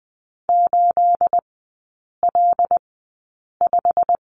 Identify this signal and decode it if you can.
Morse code, heard as a keyed tone, about 20 wpm, spaced as for 8 wpm, 700 Hz. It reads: 8L5
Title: 212 Lesson 34e - Number 8 - Three Characters 20wpm